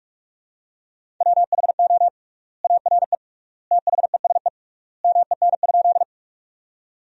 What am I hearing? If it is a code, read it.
Morse code, heard as a keyed tone, about 46 wpm, 700 Hz. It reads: WHO ARE THESE MEN?
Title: 3x Word-Spacing - Sentences from Top 200 Words - 45wpm